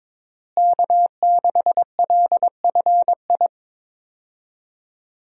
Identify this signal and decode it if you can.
Morse code, heard as a keyed tone, about 22 wpm, 700 Hz. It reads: K6LFI
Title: USA Call Signs - 5 Characters - Repeated 3x - 22wpm